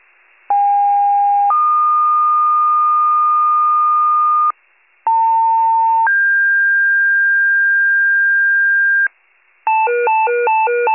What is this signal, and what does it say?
Two-tone page, heard as a one-second tone, then a three-second tone, about 810 then 1220 Hz, about 900 then 1600 Hz, with noise. It is heard from 0.5 to 9.1 s.
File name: pager / 73 RESCUE / wcalert